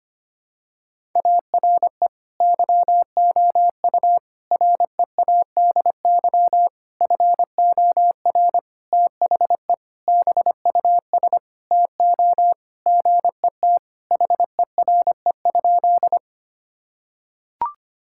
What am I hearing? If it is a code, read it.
Morse code, heard as a keyed tone, about 25 wpm, 700 Hz. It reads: ARE YOU READY FOR THE BUS TO GET HERE?
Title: Sentences from Top 1000 Words - Review Part 6 - 25wpm